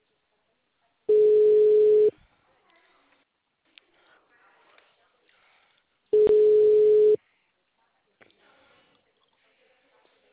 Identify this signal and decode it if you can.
Ringback tone (Europe, 425 Hz), with speech.
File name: out- -2007- - - .10588